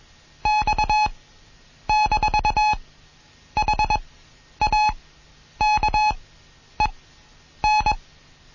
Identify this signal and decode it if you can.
Morse code, heard as a keyed tone, about 21 wpm, spaced as for 9 wpm, 850 Hz. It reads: X-HAKEN